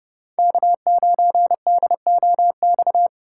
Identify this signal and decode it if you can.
Morse code, heard as a keyed tone, about 30 wpm, 700 Hz. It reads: K9DOX
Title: USA Call Signs - 5 Characters - 3x Character Spacing - 30wpm